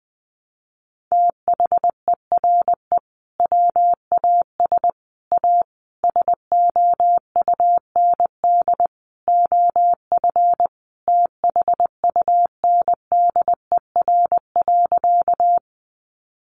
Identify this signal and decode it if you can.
Morse code, heard as a keyed tone, about 20 wpm, 700 Hz. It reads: THERE WAS A SOUND OF THUNDER.
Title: A Sound Of Thunder 20wpm